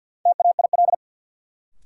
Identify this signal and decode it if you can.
Morse code, heard as a keyed tone, about 46 wpm, 700 Hz. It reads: TAIL
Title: Sentences from Top 1000 Words - Review Part 3 - New Word Drill - 50wpm